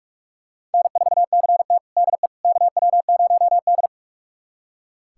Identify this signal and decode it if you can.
Morse code, heard as a keyed tone, about 41 wpm, 700 Hz. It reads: N4CT DE KW0D
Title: USA Callsign DE Callsign - 4 Characters - 45wpm